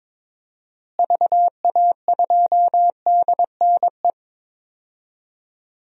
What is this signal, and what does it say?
Morse code, heard as a keyed tone, about 22 wpm, 700 Hz. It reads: VA2DNE
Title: DXCC-100 Call Signs - 6 Characters - Repeated 2x - 22wpm